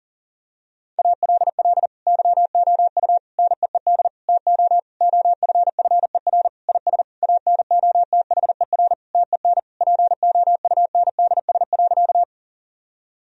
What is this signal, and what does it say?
Morse code, heard as a keyed tone, about 40 wpm, 700 Hz. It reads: ALL YOU NEED TO OFFER IS ANOTHER TEN POUNDS.
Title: Sentences from Top 1000 Words - Review Part 4 - 40wpm